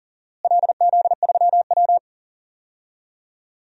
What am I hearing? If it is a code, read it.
Morse code, heard as a keyed tone, about 40 wpm, 700 Hz. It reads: LZ3W
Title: DXCC-100 Call Signs - 4 Characters - Repeated 3x - 40wpm